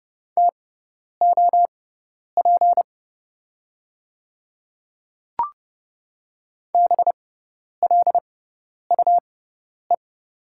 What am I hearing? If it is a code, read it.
Morse code, heard as a keyed tone, about 30 wpm, spaced as for 10 wpm, 700 Hz. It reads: TOP BLUE
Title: 6x Character Spacing - Top 500 Words - 30wpm